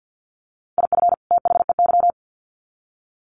Morse code, heard as a keyed tone, about 50 wpm, 700 Hz. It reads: IF THEY
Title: Sets of 2 Words 50wpm